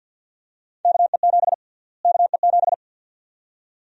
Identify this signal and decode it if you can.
Morse code, heard as a keyed tone, about 52 wpm, 700 Hz. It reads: KE7 KE7
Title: USA Call Sign Prefixes - Repeated 2x - 50wpm